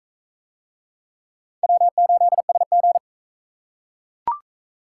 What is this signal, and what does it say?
Morse code, heard as a keyed tone, about 38 wpm, 700 Hz. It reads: W8SG